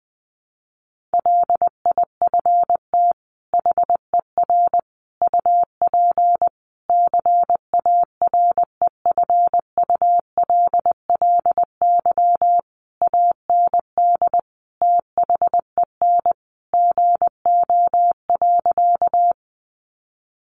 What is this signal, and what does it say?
Morse code, heard as a keyed tone, about 20 wpm, 700 Hz. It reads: LIFT HER UP CAREFULLY AND THEN GO.
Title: Sentences from Top 1300 Words 20wpm